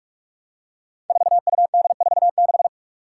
Morse code, heard as a keyed tone, about 46 wpm, 700 Hz. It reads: 4UD46
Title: Five Challenging Letter-Number Combinations 45wpm